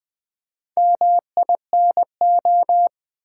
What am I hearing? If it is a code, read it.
Morse code, heard as a keyed tone, about 20 wpm, 700 Hz. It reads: MINO
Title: Build A Word - 2 Letters At A Time - 8-Letter Words - 20wpm